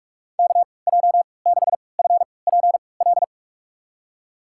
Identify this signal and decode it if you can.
Morse code, heard as a keyed tone, about 41 wpm, spaced as for 25 wpm, 690 Hz. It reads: KJ6FPL
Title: USA Call Signs - 6 Characters - 3x Character Spacing - 45wpm